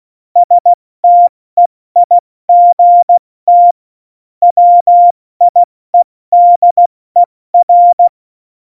Morse code, heard as a keyed tone, about 16 wpm, spaced as for 14 wpm, 700 Hz. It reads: STEIGT WIEDER
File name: Morsekurs / Witze / Witz 29. Januar 2019, 70 F80